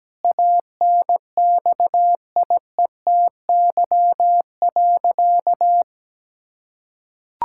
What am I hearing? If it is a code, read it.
Morse code, heard as a keyed tone, about 17 wpm, 700 Hz. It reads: ANXIETY.
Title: Sentences from Top 1600 Words 17wpm